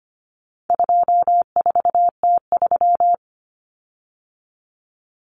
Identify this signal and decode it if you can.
Morse code, heard as a keyed tone, about 25 wpm, 700 Hz. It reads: 24T3
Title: Common Cut Numbers - 4-Digit 25wpm